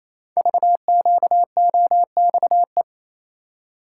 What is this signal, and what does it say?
Morse code, heard as a keyed tone, about 29 wpm, 700 Hz. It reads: VQOXE
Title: Code Groups - 5 Letters - 28wpm